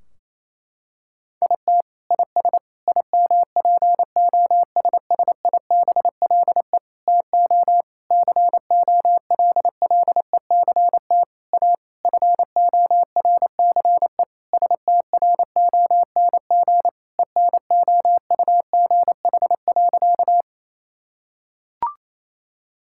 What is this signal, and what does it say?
Morse code, heard as a keyed tone, about 28 wpm, 700 Hz. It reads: IT IS IMPOSSIBLE TO COLLECT A FORCE STRONG ENOUGH.
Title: Sentences from Top 1600 Words 28wpm